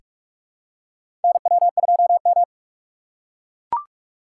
Morse code, heard as a keyed tone, about 41 wpm, 700 Hz. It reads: NW2M
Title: USA Call Signs - 4 Characters - 3x Character Spacing - 45wpm